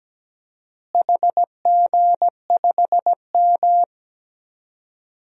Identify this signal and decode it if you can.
Morse code, heard as a keyed tone, about 17 wpm, 700 Hz. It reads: HG5M
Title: DXCC-100 Call Signs - 4 Characters - Repeated 2x - 17wpm